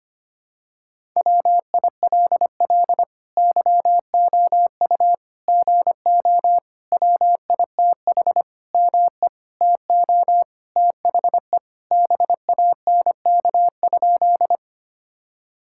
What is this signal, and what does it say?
Morse code, heard as a keyed tone, about 25 wpm, 700 Hz. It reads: WILL YOU GO WITH ME TO THE BANK?
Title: Sentences from Top 1000 Words - Review Part 6 - 25wpm